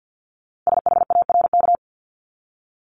Morse code, heard as a keyed tone, about 50 wpm, 690 Hz. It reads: SHARK
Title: Multi-letter Phonemes 50wpm